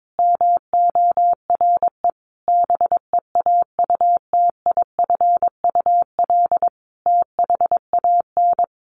Morse code, heard as a keyed tone, about 22 wpm, 700 Hz. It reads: MORE BEAUTIFUL THAN
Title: Sentences from Top 1200 Words 22wpm